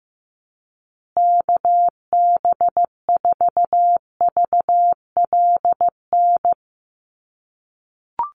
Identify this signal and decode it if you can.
Morse code, heard as a keyed tone, about 15 wpm, 700 Hz. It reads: KB4VLN
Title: USA Call Signs Repeated 2x 15wpm